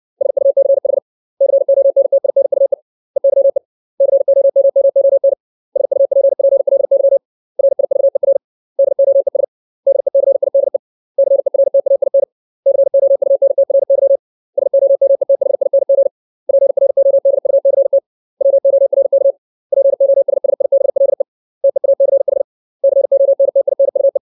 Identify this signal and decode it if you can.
Morse code, heard as a keyed tone, about 60 wpm, 550 Hz. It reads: SUCH CONTINUE EYE COMMON SUPPLY RIVER DOES DECIDE CERTAIN COUNTRY SOMETHING PRODUCT WORK POSSIBLE TEACH CONTINUE
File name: 632 words / 632 Words 60WPM